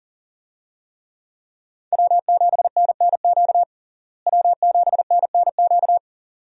Morse code, heard as a keyed tone, about 40 wpm, 700 Hz. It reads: W7NNQ W7NNQ
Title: USA Call Signs - 5 Characters - Repeated 2x - 40wpm